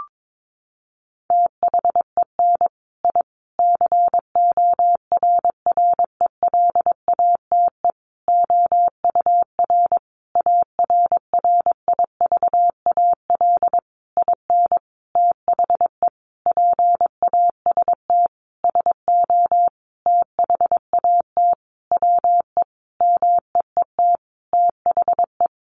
Morse code, heard as a keyed tone, about 22 wpm, 700 Hz. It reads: THEN I CORRELATE OUR ARRIVAL IN THE PAST SO THAT WE MEET THE
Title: A Sound Of Thunder 22wpm